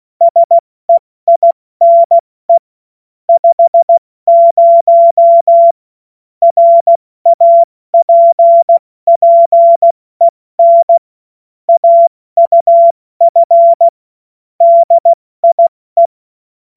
Morse code, heard as a keyed tone, about 16 wpm, 680 Hz. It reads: SEINE 50 RAPPEN AUF DIE
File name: Morsekurs / Witze / Witz 15. November 2022 70 F80